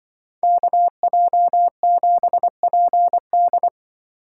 Morse code, heard as a keyed tone, about 25 wpm, 700 Hz. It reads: KJ7PD